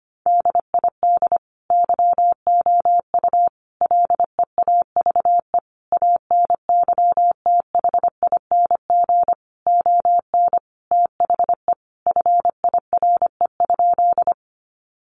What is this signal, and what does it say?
Morse code, heard as a keyed tone, about 25 wpm, 700 Hz. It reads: DID YOU LEAVE ANYTHING ON THE FIRE?
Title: Sentences from Top 1000 Words 25wpm